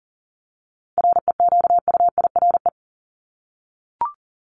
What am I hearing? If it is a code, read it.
Morse code, heard as a keyed tone, about 40 wpm, 700 Hz. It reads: REQUIRE
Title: Top 751-1000 Words 40wpm